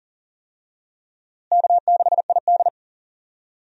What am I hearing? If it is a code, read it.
Morse code, heard as a keyed tone, about 39 wpm, 700 Hz. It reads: K6ID
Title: USA Call Signs - 4 Characters - Repeated 2x - 40wpm